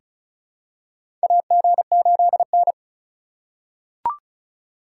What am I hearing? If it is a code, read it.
Morse code, heard as a keyed tone, about 36 wpm, 700 Hz. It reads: AG8N
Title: USA Call Signs - 4 Characters - 3x Character Spacing - 35wpm